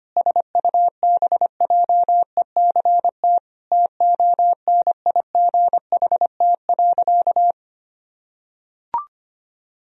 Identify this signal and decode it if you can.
Morse code, heard as a keyed tone, about 25 wpm, 700 Hz. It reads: SUBJECT TONIGHT.
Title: Sentences from Top 1000 Words - Review Part 4 - 25wpm